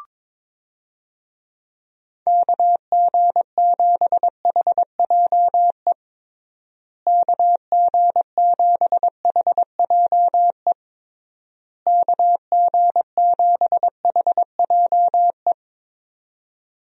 Morse code, heard as a keyed tone, about 22 wpm, 700 Hz. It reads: KG7HJE KG7HJE KG7HJE
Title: USA Call Signs - 6 Characters - Repeated 3x - 22wpm